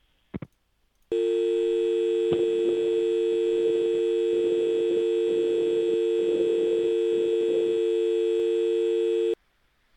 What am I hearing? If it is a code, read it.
Dial tone (UK), 350 + 450 Hz.